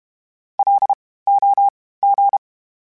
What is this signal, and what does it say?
Morse code, heard as a keyed tone, about 32 wpm, spaced as for 18 wpm, 800 Hz. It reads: LOG
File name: (spaced) / FromSMs spaced mp3s / Antennas sp@30